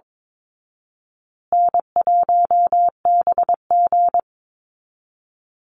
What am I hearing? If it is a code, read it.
Morse code, heard as a keyed tone, about 22 wpm, 700 Hz. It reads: N1BG
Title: Call Signs Repeated 2x - 22wpm